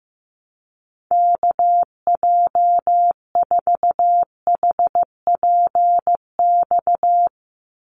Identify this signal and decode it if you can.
Morse code, heard as a keyed tone, about 15 wpm, 700 Hz. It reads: KJ4HPX
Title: USA Call Signs 15wpm